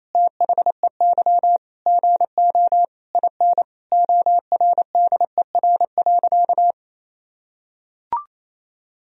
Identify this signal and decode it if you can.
Morse code, heard as a keyed tone, about 28 wpm, 700 Hz. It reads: THEY GO IN ORDER.
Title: Sentences from Top 1000 Words - Review Part 4 - 28wpm